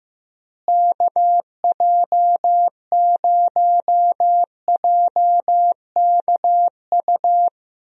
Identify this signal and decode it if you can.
Morse code, heard as a keyed tone, about 15 wpm, 700 Hz. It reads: KJ0JKU
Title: USA Call Signs - 6 Characters - Repeated 2x - 15wpm